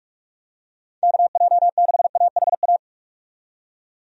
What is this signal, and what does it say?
Morse code, heard as a keyed tone, about 46 wpm, 700 Hz. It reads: KJ6AHA
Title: USA Call Signs - 6 Characters - 45wpm